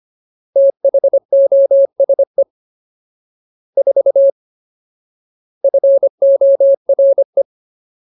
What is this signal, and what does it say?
Morse code, heard as a keyed tone, about 25 wpm, 550 Hz. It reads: THOSE 4 FORE
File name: Ultimate Word List / 25WPM Ultimate